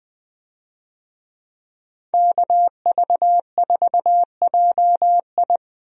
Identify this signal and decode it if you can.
Morse code, heard as a keyed tone, about 21 wpm, 700 Hz. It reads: KV4JI
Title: USA Call Signs - 5 Characters - Repeated 3x - 20wpm